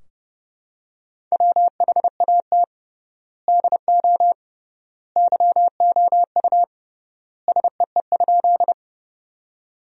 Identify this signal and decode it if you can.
Morse code, heard as a keyed tone, about 30 wpm, 700 Hz. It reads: WHAT DO YOU SEE?
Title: 3x Word-Spacing - Sentences from Top 100 Words - 30wpm